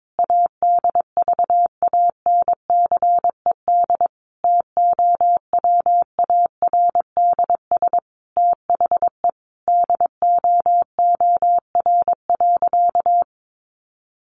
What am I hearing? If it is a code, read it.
Morse code, heard as a keyed tone, about 22 wpm, 700 Hz. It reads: ADVANCED TOWARDS THE DOOR.
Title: Sentences from Top 1200 Words 22wpm